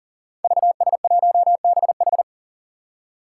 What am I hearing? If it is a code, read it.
Morse code, heard as a keyed tone, about 40 wpm, 700 Hz. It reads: VS1BH
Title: Five Challenging Letter-Number Combinations 40wpm